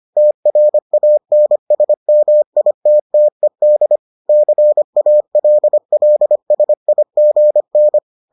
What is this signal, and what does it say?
Morse code, heard as a keyed tone, about 25 wpm, 600 Hz. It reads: TRANSMITTED CALLSIGN